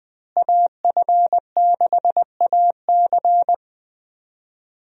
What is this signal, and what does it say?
Morse code, heard as a keyed tone, about 20 wpm, 700 Hz. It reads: AF6AC
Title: USA Call Signs - 5 Characters - Repeated 3x - 20wpm